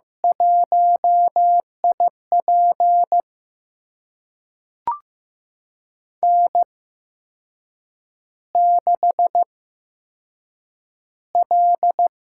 Morse code, heard as a keyed tone, about 15 wpm, spaced as for 6 wpm, 700 Hz. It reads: 1IP N6L
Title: USA Call Signs - 8x Character Spacing - 15wpm